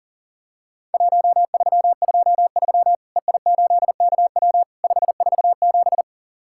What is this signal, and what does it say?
Morse code, heard as a keyed tone, about 37 wpm, 700 Hz. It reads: 1323 EI8KW 547
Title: Worked All Europe DX Contest - QTC Exchange - 40wpm